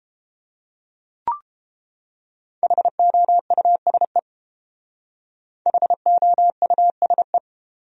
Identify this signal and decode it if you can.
Morse code, heard as a keyed tone, about 29 wpm, 700 Hz. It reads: HOUSE HOUSE